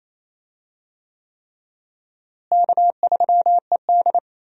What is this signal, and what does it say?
Morse code, heard as a keyed tone, about 27 wpm, 700 Hz. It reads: K3ED
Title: USA Call Signs - 4 Characters - Repeated 3x - 28wpm